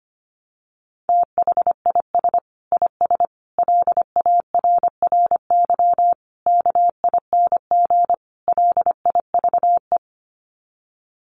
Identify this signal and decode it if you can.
Morse code, heard as a keyed tone, about 25 wpm, 700 Hz. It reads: THIS IS LARRY KING LIVE
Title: Sets of 5 Words Encore2 25wpm